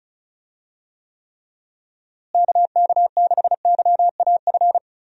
Morse code, heard as a keyed tone, about 34 wpm, 700 Hz. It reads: KK6YAF
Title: USA Call Signs - 6 Characters - Repeated 3x - 35wpm